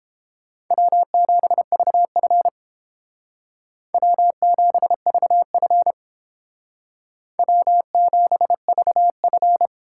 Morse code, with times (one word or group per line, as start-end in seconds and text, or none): 0.7-2.5 W7VF
3.9-5.9 W7VF
7.4-9.7 W7VF